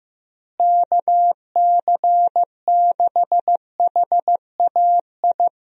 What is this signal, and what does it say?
Morse code, heard as a keyed tone, about 15 wpm, 700 Hz. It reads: KC6HAI
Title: USA Call Signs - 6 Characters - Repeated 3x - 15wpm